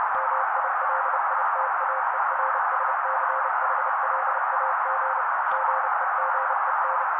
Morse code, heard as a keyed tone, about 30 wpm, 530 Hz. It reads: MILITAERSCHLAG GEGEN
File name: daily / 29wpm noise